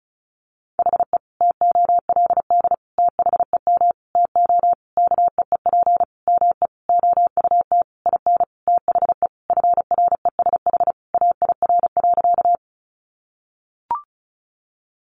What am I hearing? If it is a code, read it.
Morse code, heard as a keyed tone, about 34 wpm, 700 Hz. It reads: HE TOLD THEM TO KEEP ME OUT IN THE FRESH AIR.
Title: Sentences from Top 1200 Words 35wpm